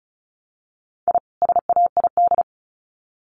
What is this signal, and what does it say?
Morse code, heard as a keyed tone, about 34 wpm, 700 Hz. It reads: I SAID
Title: Sets of 2 Words 35wpm